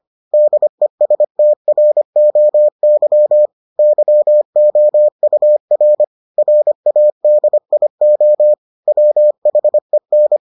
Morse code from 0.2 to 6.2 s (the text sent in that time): DESTROY YOUR